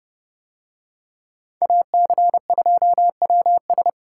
Morse code, heard as a keyed tone, about 27 wpm, 700 Hz. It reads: AC2WS